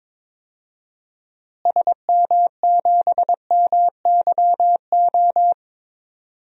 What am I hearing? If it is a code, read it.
Morse code, heard as a keyed tone, about 22 wpm, 700 Hz. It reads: SM7MYO
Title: DXCC-100 Call Signs - 6 Characters - Repeated 2x - 22wpm